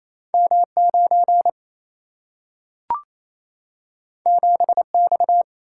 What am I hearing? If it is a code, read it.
Morse code, heard as a keyed tone, about 28 wpm, 700 Hz. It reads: M 9 7 X
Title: Single-Letter Single-Number 28wpm